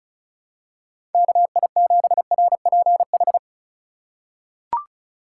Morse code, heard as a keyed tone, about 34 wpm, 700 Hz. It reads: KI7RPH
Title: USA Call Signs - 6 Characters - 3x Character Spacing - 35wpm